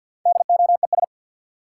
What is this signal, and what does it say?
Morse code, heard as a keyed tone, about 52 wpm, 700 Hz. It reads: DOES
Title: Sets of 2 Words - New Word Drill - 50wpm